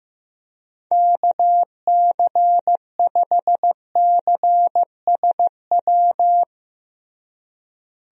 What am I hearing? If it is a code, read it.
Morse code, heard as a keyed tone, about 15 wpm, 700 Hz. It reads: KC5CSW
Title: USA Call Signs - 6 Characters - Repeated 3x - 15wpm